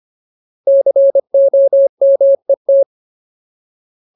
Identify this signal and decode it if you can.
Morse code, heard as a keyed tone, about 25 wpm, 550 Hz. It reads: COMET
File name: Ultimate Word List / 25WPM Ultimate